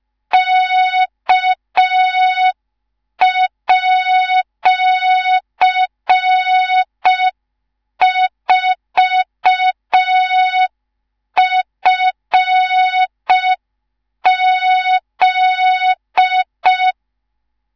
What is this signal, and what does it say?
Morse code, heard as a keyed tone, about 5 wpm, 3700 Hz. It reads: K@4FZ
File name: Morze-HA2MN / hang / 19 GYAKORLAT